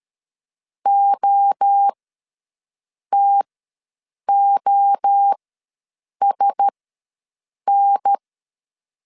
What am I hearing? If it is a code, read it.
Morse code, heard as a keyed tone, about 13 wpm, spaced as for 6 wpm, 800 Hz. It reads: OTOSN
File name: LEZIONI / 5 LBVC / 5 LB ws 12wpm 3034